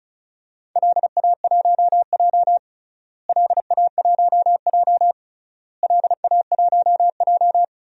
Morse code, heard as a keyed tone, about 36 wpm, 700 Hz. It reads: LA1J LA1J LA1J
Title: DXCC-100 Call Signs - 4 Characters - Repeated 3x - 35wpm